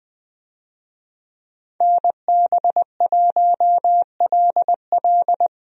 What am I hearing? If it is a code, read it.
Morse code, heard as a keyed tone, about 20 wpm, 700 Hz. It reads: NB1LL
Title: USA Call Signs - 5 Characters - Repeated 3x - 20wpm